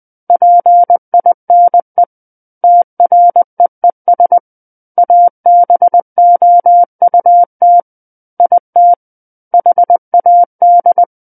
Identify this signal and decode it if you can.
Morse code, heard as a keyed tone, about 20 wpm, 700 Hz. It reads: PINE TREES ABOUT IT HAD
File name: 20 WPM